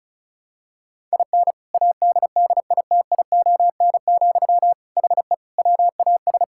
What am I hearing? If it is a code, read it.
Morse code, heard as a keyed tone, about 36 wpm, 700 Hz. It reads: IN ADDITION, HE WAS
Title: Sentences from Top 1000 Words - Review Part 6 - 35wpm